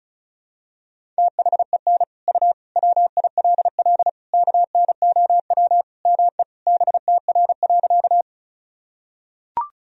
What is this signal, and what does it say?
Morse code, heard as a keyed tone, about 36 wpm, 700 Hz. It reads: THEN U WILL KNOW ME BTR.
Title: Sentences from Top 400 Words and Abbreviations 35wpm